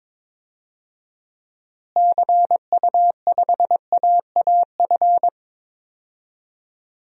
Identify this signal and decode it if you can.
Morse code, heard as a keyed tone, about 22 wpm, 700 Hz. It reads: CU5AAF
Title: DXCC-100 Call Signs - 6 Characters - Repeated 3x - 22wpm